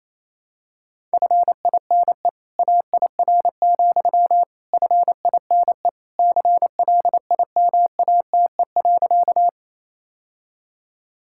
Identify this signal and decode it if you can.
Morse code, heard as a keyed tone, about 28 wpm, 700 Hz. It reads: FINE AIR, FINE CLIMATE.
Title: Sentences from Top 1600 Words 28wpm